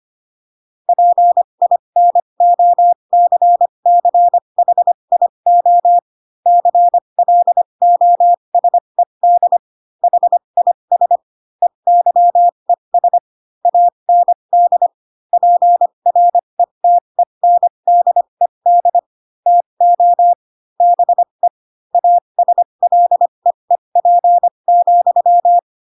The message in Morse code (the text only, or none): PINOCCHIO CLOSED HIS EYES AND PRETENDED TO BE ASLEEP,